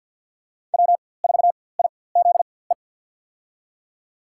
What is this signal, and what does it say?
Morse code, heard as a keyed tone, about 46 wpm, spaced as for 23 wpm, 700 Hz. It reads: W4IZE
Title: USA Call Signs - 5 Characters - 4x Character Spacing - 50wpm